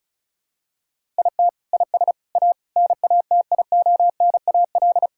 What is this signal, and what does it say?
Morse code, heard as a keyed tone, about 36 wpm, 700 Hz. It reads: IT IS A NATIONAL
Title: Sentences from Top 1000 Words - Review Part 5 - 35wpm